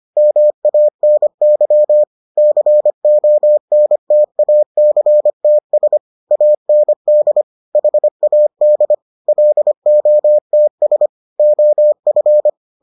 Morse code, heard as a keyed tone, about 25 wpm, 600 Hz. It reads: MANY CONTACTS AND HAD LOTS OF